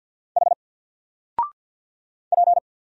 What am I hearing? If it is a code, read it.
Morse code, heard as a keyed tone, about 50 wpm, 700 Hz. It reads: H P
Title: Single Letter-Number 50wpm